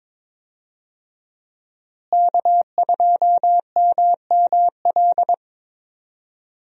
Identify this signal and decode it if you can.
Morse code, heard as a keyed tone, about 22 wpm, 700 Hz. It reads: K2MML
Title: USA Call Signs - 5 Characters - Repeated 3x - 22wpm